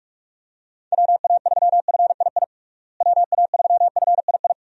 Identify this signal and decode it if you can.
Morse code, heard as a keyed tone, about 44 wpm, 700 Hz. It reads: WA3FII WA3FII
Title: USA Call Signs - 6 Characters - Repeated 2x - 45wpm